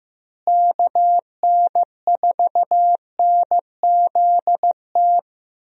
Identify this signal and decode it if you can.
Morse code, heard as a keyed tone, about 15 wpm, 700 Hz. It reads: KN4NZT
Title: USA Call Signs - 6 Characters - Repeated 3x - 15wpm